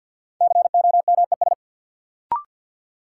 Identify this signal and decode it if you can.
Morse code, heard as a keyed tone, about 52 wpm, 700 Hz. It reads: COMES
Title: Sentences from Top 1000 Words - Review Part 5 - New Word Drill - 50wpm